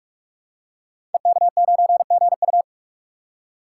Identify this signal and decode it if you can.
Morse code, heard as a keyed tone, about 48 wpm, 700 Hz. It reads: EK9GU
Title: Code Groups - 5 Letter:Numbers - 45wpm